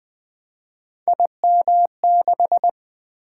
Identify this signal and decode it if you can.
Morse code, heard as a keyed tone, about 20 wpm, 700 Hz. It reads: IM6
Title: Three Letter-Number Combinations - 20wpm